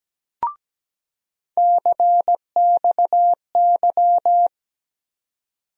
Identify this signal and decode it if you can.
Morse code, heard as a keyed tone, about 17 wpm, 700 Hz. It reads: CXY
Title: USA Call Sign Postfixes - 3 Letters 17wpm